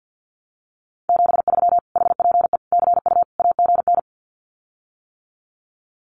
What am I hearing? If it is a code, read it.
Morse code, heard as a keyed tone, about 50 wpm, 700 Hz. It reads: 73 HPE CU AGN
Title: Common QSO Phrases Encore 50wpm